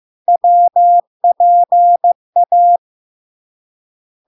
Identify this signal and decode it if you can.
Morse code, heard as a keyed tone, about 15 wpm, 700 Hz. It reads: WPA
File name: LARK-FD-2022-Log-in-CW-15wpm-700-Hz